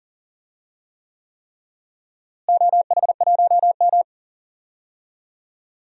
Morse code, heard as a keyed tone, about 40 wpm, 700 Hz. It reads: OH1M
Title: DXCC-100 Call Signs - 4 Characters - Repeated 3x - 40wpm